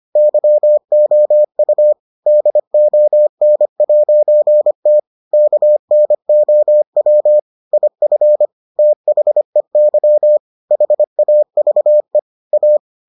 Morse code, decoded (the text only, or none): YOU DON'T KNOW IF THEY HAVE A